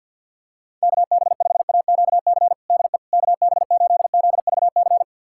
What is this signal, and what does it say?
Morse code, heard as a keyed tone, about 50 wpm, 700 Hz. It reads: KB5AQC DE KB8ZVC
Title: USA Callsign DE Callsign - 6 Characters - 50wpm